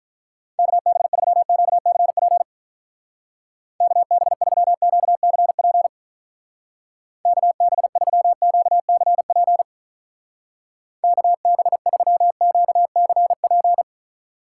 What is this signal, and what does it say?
Morse code, heard as a keyed tone, about 46 wpm, 700 Hz. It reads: KB3QCP KB3QCP KB3QCP KB3QCP